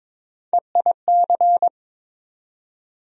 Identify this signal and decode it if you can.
Morse code, heard as a keyed tone, about 22 wpm, 700 Hz. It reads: EIC
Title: USA Call Sign Postfixes - 3 Letters 22wpm